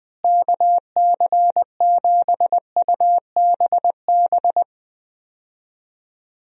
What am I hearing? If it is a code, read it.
Morse code, heard as a keyed tone, about 20 wpm, 700 Hz. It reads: KC7UBB
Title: USA Call Signs - 6 Characters - Repeated 2x - 20wpm